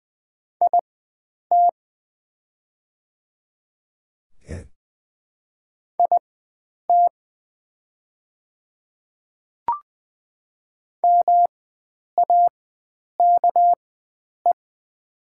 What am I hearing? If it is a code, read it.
Morse code, heard as a keyed tone, about 20 wpm, spaced as for 9 wpm, 700 Hz. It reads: IT IT MAKE
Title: 4x Character Spacing - Top 100 Words - 20wpm